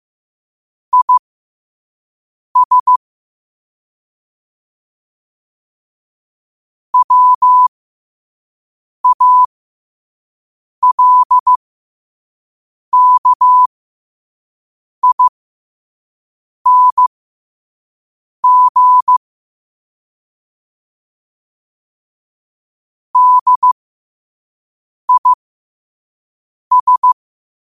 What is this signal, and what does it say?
Morse code, heard as a keyed tone, about 15 wpm, spaced as for 5 wpm, 1000 Hz. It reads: IS WALKING DIS